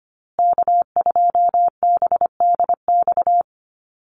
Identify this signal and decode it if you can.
Morse code, heard as a keyed tone, about 25 wpm, 700 Hz. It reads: K2BDX
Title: USA Call Signs Repeated 2x 25wpm